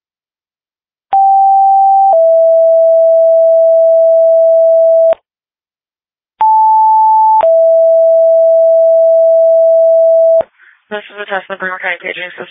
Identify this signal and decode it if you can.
Two-tone page, heard as a one-second tone, then a three-second tone, about 780 then 660 Hz, about 880 then 660 Hz, with speech. It is heard from 1.1 to 10.4 s.
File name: dispatch / Frederika Fire 2026 09 03 19 03 39